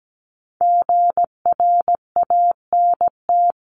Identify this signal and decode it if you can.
Morse code, heard as a keyed tone, about 17 wpm, 700 Hz. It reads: GRANT
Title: CWT Exchange 17wpm